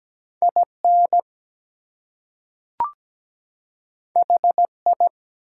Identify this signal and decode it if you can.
Morse code, heard as a keyed tone, about 17 wpm, 700 Hz. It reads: IN HI
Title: Two Letter Words - 17wpm